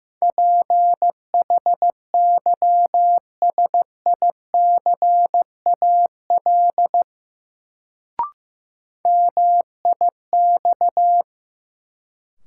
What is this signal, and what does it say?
Morse code, heard as a keyed tone, about 15 wpm, 700 Hz. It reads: PHYSICAL MIX
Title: Sentences from Top 1000 Words - Review Part 4 - New Word Drill - 15wpm